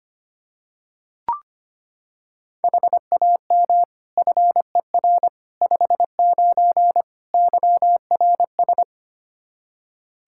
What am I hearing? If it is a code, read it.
Morse code, heard as a keyed tone, about 25 wpm, 700 Hz. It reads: HAM FER 59 YRS
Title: Practice QSO - Exchange 3 - More Info - 25wpm